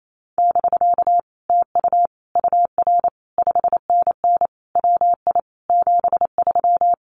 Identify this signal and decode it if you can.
Morse code, heard as a keyed tone, about 28 wpm, 700 Hz. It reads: <BK> TU UR 5NN WI 73
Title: Common QSO Phrases Encore 28wpm